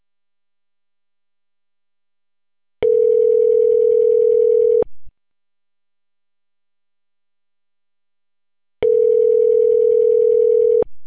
Ringback tone (North America), 440 + 480 Hz.